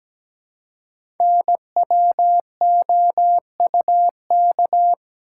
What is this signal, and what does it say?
Morse code, heard as a keyed tone, about 17 wpm, 700 Hz. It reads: NWOUK
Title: Code Groups - 5 Letters - 17wpm